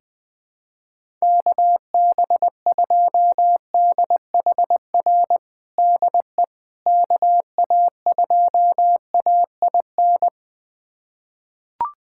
Morse code, heard as a keyed tone, about 20 wpm, 700 Hz. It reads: KB2DHR DE KA2AIN
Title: USA Callsign DE Callsign - 6 Characters - 20wpm